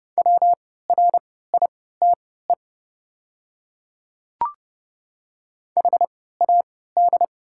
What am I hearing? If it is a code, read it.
Morse code, heard as a keyed tone, about 29 wpm, spaced as for 17 wpm, 700 Hz. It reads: WRITE HAD
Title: 3x Character Spacing - Top 200 Words - 30wpm